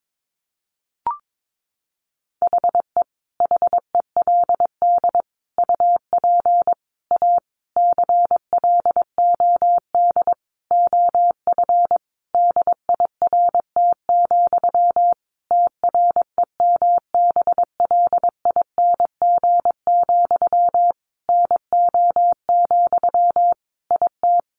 Morse code, heard as a keyed tone, about 22 wpm, 700 Hz. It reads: HE HELD UP A CLOD OF DIRT, TREMBLING, NO, IT
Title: A Sound Of Thunder 22wpm